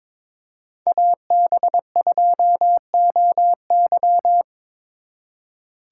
Morse code, heard as a keyed tone, about 26 wpm, 700 Hz. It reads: AB2OY
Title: Speed Racing - USA 2x2 Call Signs 22wpm